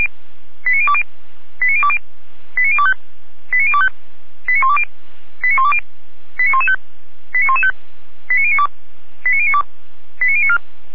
Five-tone selective call (ZVEI), heard as quick runs of five tones, tones of about 70 ms. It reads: 89020 89020 89025 89025 89120 89120 89115 89115 89092 89092 89094
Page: SDR/ZVEI